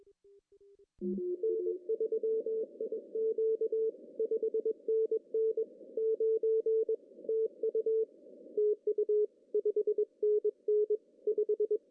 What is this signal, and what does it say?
Morse code, heard as a keyed tone, about 21 wpm, 440 Hz. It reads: N3IQ 5NN 9 TU TU 5NN 5
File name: CW / cqww-cw-k5tr